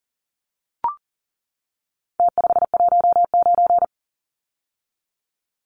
Morse code, heard as a keyed tone, about 40 wpm, 700 Hz. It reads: T519
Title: Numbers - Contest Style - 4-Digit 40wpm